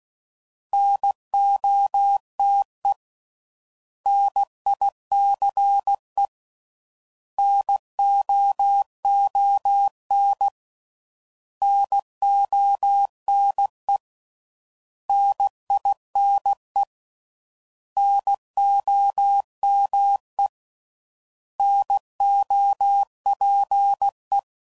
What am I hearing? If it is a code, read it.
Morse code, heard as a keyed tone, about 16 wpm, 800 Hz. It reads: NOTE NICE NOON NONE NINE NOME NOPE